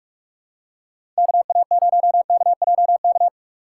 Morse code, heard as a keyed tone, about 46 wpm, 700 Hz. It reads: KA0KJK
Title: USA Call Signs - 6 Characters - 2x Character Spacing - 45wpm